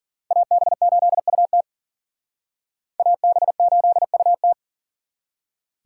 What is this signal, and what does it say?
Morse code, heard as a keyed tone, about 51 wpm, 700 Hz. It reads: AB8UT AB8UT